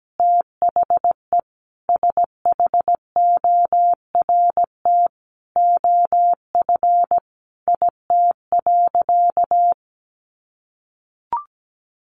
Morse code, heard as a keyed tone, about 17 wpm, 700 Hz. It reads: THE SHORT OF IT.
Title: Sentences from Top 400 Words 17wpm